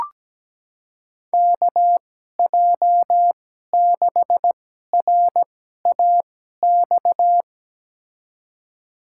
Morse code, heard as a keyed tone, about 17 wpm, spaced as for 12 wpm, 700 Hz. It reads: KJ6RAX
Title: USA Call Signs - 6 Characters - 2x Character Spacing - 17wpm